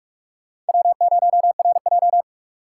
Morse code, heard as a keyed tone, about 56 wpm, 700 Hz. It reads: W0RJ
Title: Speed Racing - USA 1x2 Call Signs 45wpm